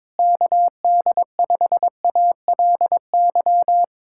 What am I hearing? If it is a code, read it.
Morse code, heard as a keyed tone, about 22 wpm, 700 Hz. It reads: KD5ALY
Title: USA Callsign DE Callsign 22wpm